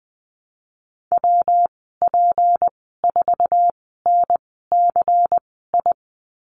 Morse code, heard as a keyed tone, about 20 wpm, spaced as for 15 wpm, 700 Hz. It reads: WP4NCI
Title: 179 Lesson 29g - Slash - Call Signs 20wpm